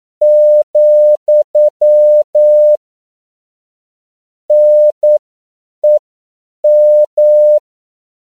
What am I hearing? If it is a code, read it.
Morse code, heard as a keyed tone, about 9 wpm, spaced as for 7 wpm, 600 Hz. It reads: , NEM
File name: CW / 10PPM